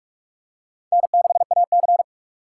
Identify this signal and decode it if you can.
Morse code, heard as a keyed tone, about 46 wpm, 700 Hz. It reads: N6AC
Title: USA Call Signs - 4 Characters - 3x Character Spacing - 45wpm